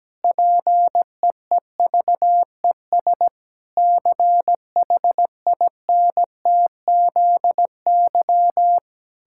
Morse code, heard as a keyed tone, about 17 wpm, 700 Hz. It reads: PEEVES CHINTZY
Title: Pangram Challenge v2 - 17wpm